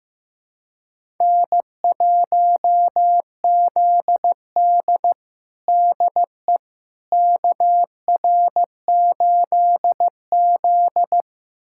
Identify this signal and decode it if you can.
Morse code, heard as a keyed tone, about 15 wpm, 700 Hz. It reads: N1ZD DE KR8Z
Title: USA Callsign DE Callsign - 4 Characters - 15wpm